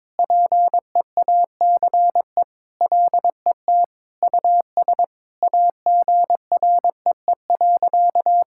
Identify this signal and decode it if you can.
Morse code, heard as a keyed tone, about 22 wpm, 700 Hz. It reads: PEACE LET US AGREE.
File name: practice / 22wpm / Sentences with 4-Word Phrases - 22wpm